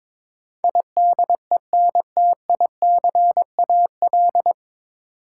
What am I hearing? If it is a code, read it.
Morse code, heard as a keyed tone, about 22 wpm, 700 Hz. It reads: IDENTICAL
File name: 9 Letter Words - First 3 Letters Given - 22wpm